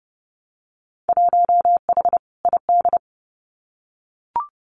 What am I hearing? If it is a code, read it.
Morse code, heard as a keyed tone, about 30 wpm, 700 Hz. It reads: 1H ID
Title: Winter Field Day Exchanges 30wpm